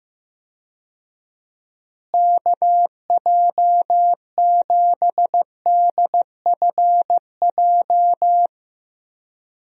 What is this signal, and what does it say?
Morse code, heard as a keyed tone, about 15 wpm, 700 Hz. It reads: KJ7DFJ
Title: USA Call Signs - 6 Characters - Repeated 2x - 15wpm